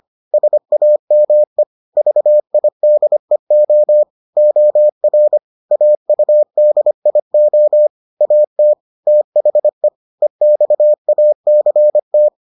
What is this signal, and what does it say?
Morse code, heard as a keyed tone, about 25 wpm, 600 Hz. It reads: SAME VIDEO OR AUDIO AT THE EXACT